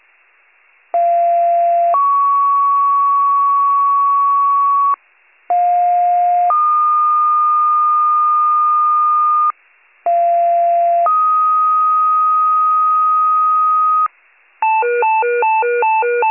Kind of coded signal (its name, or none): two-tone page